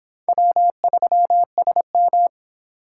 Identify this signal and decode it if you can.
Morse code, heard as a keyed tone, about 25 wpm, 700 Hz. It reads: W3SM